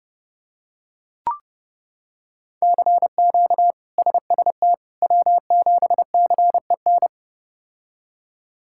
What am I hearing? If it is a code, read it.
Morse code, heard as a keyed tone, about 30 wpm, 700 Hz. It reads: CQ SST W7CEN
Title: K1USN SST QSO Practice - 30wpm